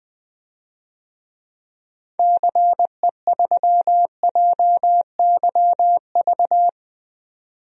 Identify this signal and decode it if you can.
Morse code, heard as a keyed tone, about 20 wpm, 700 Hz. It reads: CE3JYV
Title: DXCC-100 Call Signs - 6 Characters - 20wpm